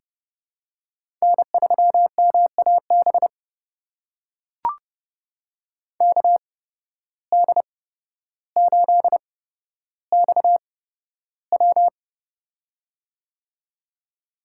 Morse code, heard as a keyed tone, about 30 wpm, spaced as for 13 wpm, 700 Hz. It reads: N3MAB KD8XW
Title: USA Call Signs - 5 Characters - 8x Character Spacing - 30wpm